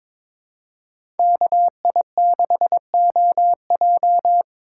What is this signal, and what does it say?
Morse code, heard as a keyed tone, about 22 wpm, 700 Hz. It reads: KI6OJ
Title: USA Call Signs - 5 Characters - 3x Character Spacing - 22wpm